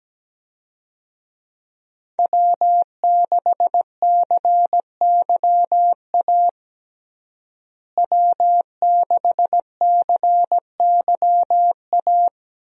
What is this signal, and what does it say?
Morse code, heard as a keyed tone, about 17 wpm, 700 Hz. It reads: W6CYA W6CYA
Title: USA Call Signs - 5 Characters - Repeated 2x - 17wpm